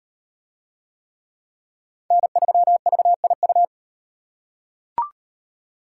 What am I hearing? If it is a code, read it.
Morse code, heard as a keyed tone, about 36 wpm, 700 Hz. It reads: N3VIU